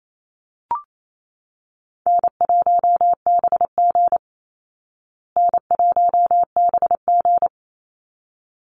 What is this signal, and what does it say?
Morse code, heard as a keyed tone, about 28 wpm, 700 Hz. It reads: N1BG N1BG
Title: Easy Call Signs Repeated 2x - 28wpm